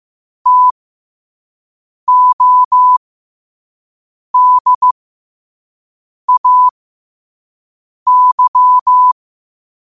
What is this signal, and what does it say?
Morse code, heard as a keyed tone, about 15 wpm, spaced as for 5 wpm, 1000 Hz. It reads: TODAY
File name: qotdfiles / 5wpm